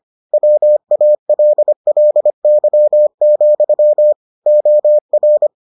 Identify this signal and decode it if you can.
Morse code, heard as a keyed tone, about 25 wpm, 600 Hz. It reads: WALLY, OR